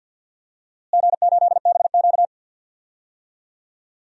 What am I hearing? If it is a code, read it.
Morse code, heard as a keyed tone, about 46 wpm, 700 Hz. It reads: G8BQ
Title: DXCC-100 Call Signs - 4 Characters - Repeated 3x - 50wpm